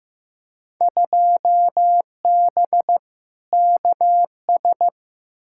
Morse code, heard as a keyed tone, about 15 wpm, 700 Hz. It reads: 2B KS
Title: Field Day Exchanges Encore - 15wpm